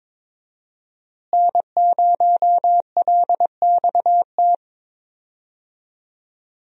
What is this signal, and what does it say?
Morse code, heard as a keyed tone, about 22 wpm, 700 Hz. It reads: N0LXT
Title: USA Call Signs - 5 Characters - Repeated 3x - 22wpm